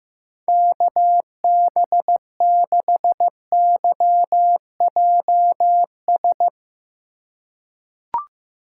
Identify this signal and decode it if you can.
Morse code, heard as a keyed tone, about 15 wpm, 700 Hz. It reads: KB6YJS
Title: USA Call Signs - 4x Character Spacing - 15wpm